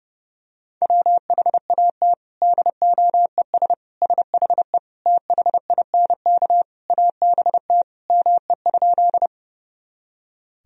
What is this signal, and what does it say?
Morse code, heard as a keyed tone, about 32 wpm, 700 Hz. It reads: WHAT DOES SHE THINK ABT ME?
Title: Sentences from Top 200 Words and Abbreviations 30wpm